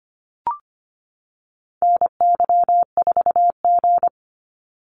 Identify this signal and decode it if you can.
Morse code, heard as a keyed tone, about 25 wpm, 700 Hz. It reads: NY4G
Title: NA SOTA Activator Callsigns 25wpm